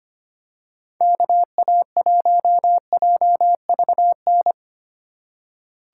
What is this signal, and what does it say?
Morse code, heard as a keyed tone, about 25 wpm, 700 Hz. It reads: KA1JVN
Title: USA Call Signs - 3x Character Spacing - 25wpm